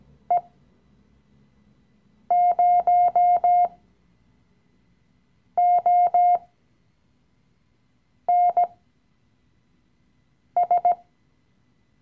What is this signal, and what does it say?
Morse code, heard as a keyed tone, about 17 wpm, spaced as for 4 wpm, 710 Hz. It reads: E0ONS
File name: CW-Online-Kurs / Anfänger / Lektion 015